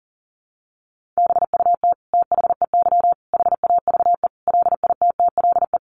Morse code, heard as a keyed tone, about 40 wpm, 700 Hz. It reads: BUT THEY HAVE LITTLE
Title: Sentences from Top 1200 Words 40wpm